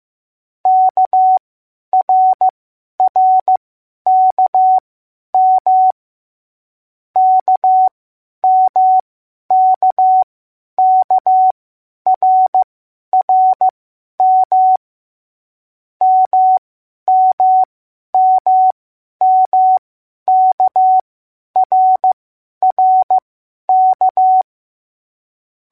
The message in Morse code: KRRKM KMKKRRM MMMMKRRK